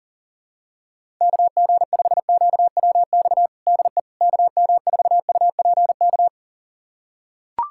Morse code, heard as a keyed tone, about 40 wpm, 700 Hz. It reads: KG5QWX DE KM4UPK
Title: USA Callsign DE Callsign - 6 Characters - 40wpm